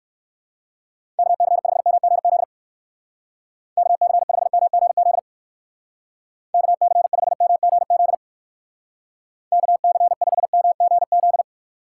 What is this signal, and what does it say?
Morse code, heard as a keyed tone, about 60 wpm, 700 Hz. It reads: KC5MGZ KC5MGZ KC5MGZ KC5MGZ